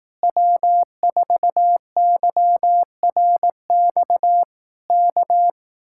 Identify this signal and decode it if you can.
Morse code, heard as a keyed tone, about 18 wpm, 700 Hz. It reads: W4YRX K
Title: Speed-Racing Common QSO Phrases Encore - 15wpm